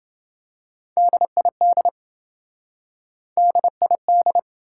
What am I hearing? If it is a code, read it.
Morse code, heard as a keyed tone, about 28 wpm, 700 Hz. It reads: DID DID